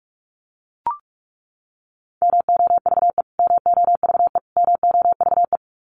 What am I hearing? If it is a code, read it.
Morse code, heard as a keyed tone, about 44 wpm, 700 Hz. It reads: MOVE MOVE MOVE
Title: Top 300 Words Repeated 45wpm